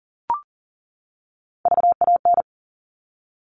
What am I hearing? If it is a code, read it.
Morse code, heard as a keyed tone, about 40 wpm, 700 Hz. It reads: VAN
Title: All Cut Numbers - 3-Digit 40wpm